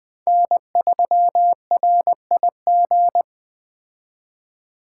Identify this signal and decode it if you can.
Morse code, heard as a keyed tone, about 20 wpm, 700 Hz. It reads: N3RIG
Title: Speed Racing - USA 1x3 Call Signs 17wpm